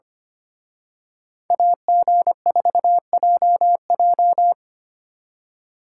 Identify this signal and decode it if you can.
Morse code, heard as a keyed tone, about 29 wpm, 700 Hz. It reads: AG4JJ
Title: Speed Racing - USA 2x2 Call Signs 25wpm